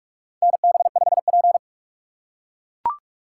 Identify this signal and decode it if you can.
Morse code, heard as a keyed tone, about 42 wpm, 700 Hz. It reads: NB5P
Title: USA Call Signs - 3x Character Spacing - 45wpm